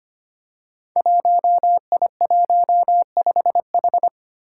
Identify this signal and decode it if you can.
Morse code, heard as a keyed tone, about 26 wpm, 700 Hz. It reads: 1I15H
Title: Five Challenging Letter-Number Combinations 25wpm